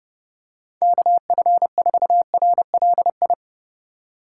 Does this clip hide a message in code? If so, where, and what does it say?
Yes: Morse code from 0.8 to 3.3 s, heard as a keyed tone, about 29 wpm, 700 Hz. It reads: KF4RLI